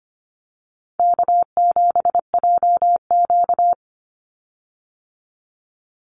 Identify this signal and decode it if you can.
Morse code, heard as a keyed tone, about 25 wpm, 700 Hz. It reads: K7JQ
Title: Easy Call Signs Repeated 2x - 25wpm